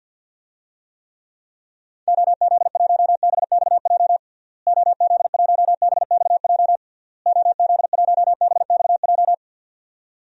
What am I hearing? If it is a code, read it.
Morse code, heard as a keyed tone, about 51 wpm, 700 Hz. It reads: OZ1BXJ OZ1BXJ OZ1BXJ
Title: DXCC-100 Call Signs - 6 Characters - Repeated 3x - 50wpm